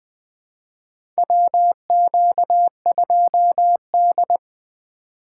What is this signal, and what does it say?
Morse code, heard as a keyed tone, about 20 wpm, 700 Hz. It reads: WQ2D
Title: USA Call Signs - 2x Character Spacing - 20wpm